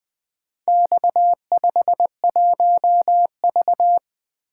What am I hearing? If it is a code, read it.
Morse code, heard as a keyed tone, about 24 wpm, 700 Hz. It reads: X51V
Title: Speed-Racing - Four Letter-Number Combinations - 20wpm